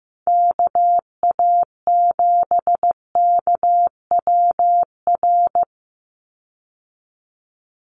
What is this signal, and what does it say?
Morse code, heard as a keyed tone, about 15 wpm, 700 Hz. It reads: KA7KWR
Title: USA Call Signs 15wpm